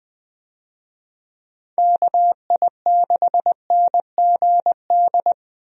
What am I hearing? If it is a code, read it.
Morse code, heard as a keyed tone, about 20 wpm, 700 Hz. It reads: KI6NGD
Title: USA Call Signs - 6 Characters - Repeated 3x - 20wpm